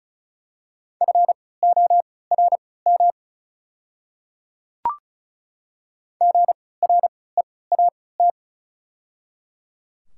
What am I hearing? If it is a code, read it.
Morse code, heard as a keyed tone, about 34 wpm, spaced as for 20 wpm, 700 Hz. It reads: FORM GREAT
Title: 3x Character Spacing - Top 200 Words - 35wpm